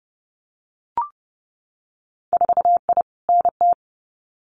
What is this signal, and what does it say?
Morse code, heard as a keyed tone, about 30 wpm, 700 Hz. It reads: 4I NT
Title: Winter Field Day Exchanges Encore 30wpm